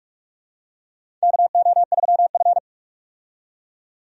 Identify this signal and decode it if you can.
Morse code, heard as a keyed tone, about 46 wpm, 700 Hz. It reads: KO3F
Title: USA Call Signs - 4 Characters - 45wpm